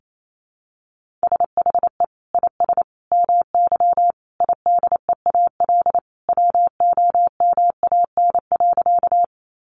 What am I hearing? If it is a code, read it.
Morse code, heard as a keyed tone, about 28 wpm, 700 Hz. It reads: SHE IS MY IDEAL WOMAN.
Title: Sentences from Top 1400 Words 28wpm